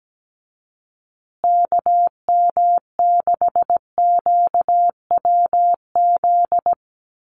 Morse code, heard as a keyed tone, about 17 wpm, 700 Hz. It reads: KM6QWZ
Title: USA Call Signs Repeated 2x 17wpm